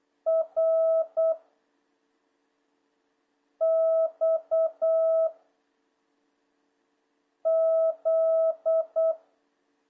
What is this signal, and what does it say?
Morse code, heard as a keyed tone, about 8 wpm, 660 Hz. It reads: R X Z
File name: jh8cbh / 2026cw